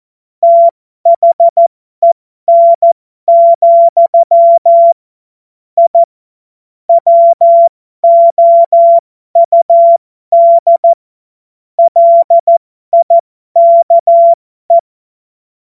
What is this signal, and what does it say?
Morse code, heard as a keyed tone, about 14 wpm, spaced as for 12 wpm, 680 Hz. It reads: THEN, I WOUD LIKE
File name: Morsekurs / Witze